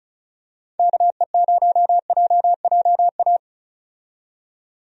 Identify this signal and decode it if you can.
Morse code, heard as a keyed tone, about 34 wpm, 700 Hz. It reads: KE0JJA
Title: USA Call Signs - 6 Characters - 35wpm